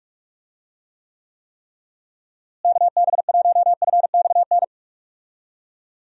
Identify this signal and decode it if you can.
Morse code, heard as a keyed tone, about 46 wpm, 700 Hz. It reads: KB1FXN
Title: USA Call Signs - 6 Characters - Repeated 3x - 45wpm